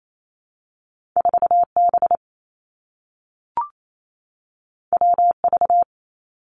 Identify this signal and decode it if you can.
Morse code, heard as a keyed tone, about 28 wpm, 700 Hz. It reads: 4 B W V
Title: Two Challenging Letter-Number Combinations 28wpm